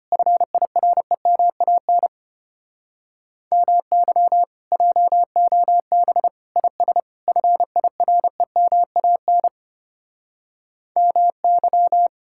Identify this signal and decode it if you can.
Morse code, heard as a keyed tone, about 32 wpm, 700 Hz. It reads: FIREMAN MY JOB IS FIREMAN MY